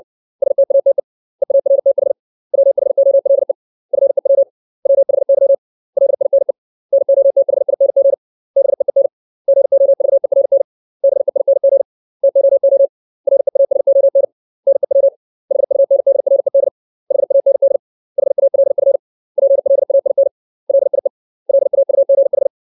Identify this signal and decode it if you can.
Morse code, heard as a keyed tone, about 60 wpm, 550 Hz. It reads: STATE EARTH WHOLE FEW WHY LINE NOTHING BEEN GOVERN BEING TOO REASON NEW HUNDRED HAND HALF PLAIN LIE LAUGH
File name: 632 words / 632 Words 60WPM